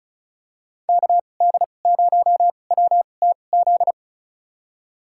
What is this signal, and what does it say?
Morse code, heard as a keyed tone, about 34 wpm, spaced as for 25 wpm, 700 Hz. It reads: KD0WTZ
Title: USA Call Signs - 6 Characters - 2x Character Spacing - 35wpm